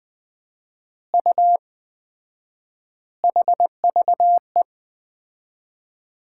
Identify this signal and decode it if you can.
Morse code, heard as a keyed tone, about 20 wpm, 700 Hz. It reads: U HVE
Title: Common QSO Phrases - 4x Word-Spacing - 20wpm